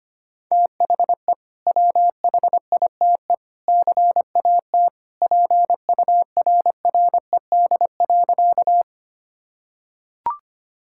Morse code, heard as a keyed tone, about 25 wpm, 700 Hz. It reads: THE WHITE CAT PURRED.
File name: practice / 25wpm / Sentences with 2-Word Phrases - 25wpm